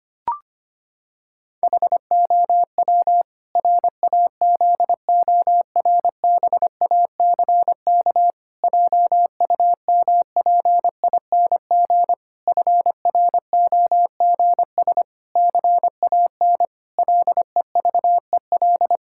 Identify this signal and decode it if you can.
Morse code, heard as a keyed tone, about 25 wpm, 700 Hz. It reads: HOW RAZORBACK JUMPING FROGS CAN LEVEL
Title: Pangram Challenge v2 - 25wpm